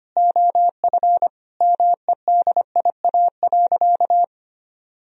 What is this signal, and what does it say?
Morse code, heard as a keyed tone, about 25 wpm, 700 Hz. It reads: OF MEDIA.
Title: Sentences from Top 1000 Words - Review Part 4 - 25wpm